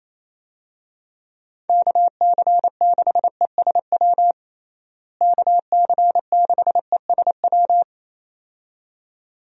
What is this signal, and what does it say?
Morse code, heard as a keyed tone, about 27 wpm, 700 Hz. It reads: KC6ESW KC6ESW
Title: USA Call Signs - 6 Characters - Repeated 2x - 28wpm